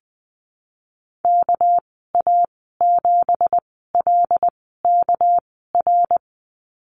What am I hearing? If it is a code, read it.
Morse code, heard as a keyed tone, about 20 wpm, spaced as for 15 wpm, 700 Hz. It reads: KA7LKR
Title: 238 Lesson 38g - Letter q - Call Signs 20wpm